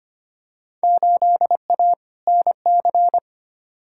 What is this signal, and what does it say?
Morse code, heard as a keyed tone, about 25 wpm, 700 Hz. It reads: 8A NC
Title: Field Day Exchanges - 25wpm